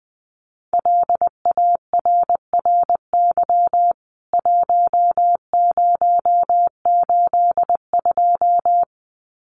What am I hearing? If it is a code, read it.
Morse code, heard as a keyed tone, about 20 wpm, 700 Hz. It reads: LARRY 1082
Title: CWT Exchange 20wpm